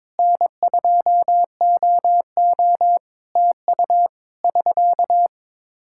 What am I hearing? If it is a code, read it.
Morse code, heard as a keyed tone, about 22 wpm, 700 Hz. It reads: N2OO TU <SK>
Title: Practice QSO - Exchange 4 - Ending - 22wpm